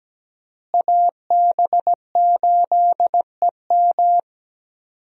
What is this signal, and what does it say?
Morse code, heard as a keyed tone, about 17 wpm, 700 Hz. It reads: AB8EM
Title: USA Call Signs - 5 Characters - Repeated 2x - 17wpm